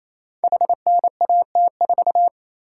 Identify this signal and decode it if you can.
Morse code, heard as a keyed tone, about 27 wpm, 700 Hz. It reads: HNAT4
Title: Code Groups - 5 Letter:Numbers - 28wpm